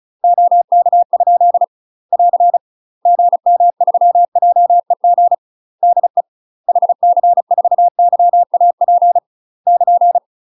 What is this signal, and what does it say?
Morse code, heard as a keyed tone, about 36 wpm, 700 Hz. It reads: OK? <AR> GM3JEG DE HC4YAP <KN>
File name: UK MASTER QSOs TRAINING / 35WPM / UK MASTER QSOs 0036